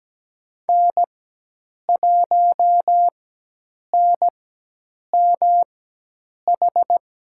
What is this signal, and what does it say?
Morse code, heard as a keyed tone, about 17 wpm, spaced as for 8 wpm, 700 Hz. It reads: N1NMH
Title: USA Call Signs - 5 Characters - 4x Character Spacing - 17wpm